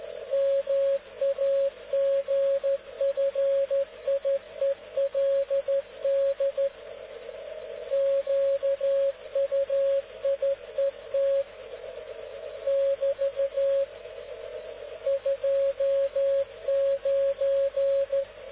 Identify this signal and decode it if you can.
Morse code, heard as a keyed tone, about 13 wpm, 550 Hz. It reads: MAGFIELD QUIET <BT> 29